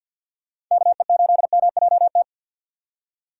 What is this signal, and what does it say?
Morse code, heard as a keyed tone, about 47 wpm, 700 Hz. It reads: KE8MJT
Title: USA Call Signs - 6 Characters - 6x Character Spacing - 50wpm